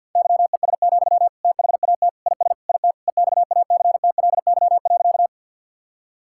Morse code, heard as a keyed tone, about 50 wpm, 700 Hz. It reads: YES, THAT IS IT EXACTLY.
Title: Sentences from Top 1000 Words - Review Part 6 - 50wpm